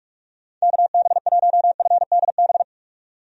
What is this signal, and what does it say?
Morse code, heard as a keyed tone, about 46 wpm, 700 Hz. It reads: KB1FDB
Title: USA Call Signs - 6 Characters - 45wpm